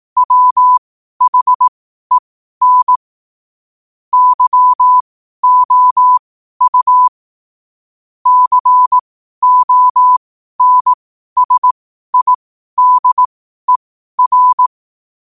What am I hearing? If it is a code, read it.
Morse code, heard as a keyed tone, about 18 wpm, spaced as for 12 wpm, 1000 Hz. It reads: WHEN YOU CONSIDER
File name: qotdfiles / 12wpm